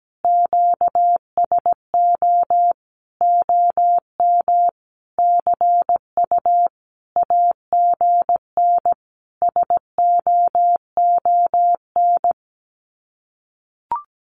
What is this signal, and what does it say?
Morse code, heard as a keyed tone, about 17 wpm, 700 Hz. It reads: QSO OM CU AGN SOON
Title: Common QSO Phrases 17wpm